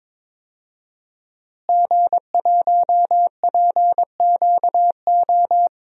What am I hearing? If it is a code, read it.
Morse code, heard as a keyed tone, about 21 wpm, 700 Hz. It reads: G1PQO
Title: DXCC-100 Call Signs - 5 Characters - Repeated 3x - 22wpm